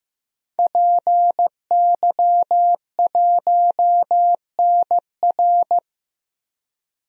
Morse code, heard as a keyed tone, about 15 wpm, 700 Hz. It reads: PY1NR
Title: DXCC-100 Call Signs - 5 Characters - 15wpm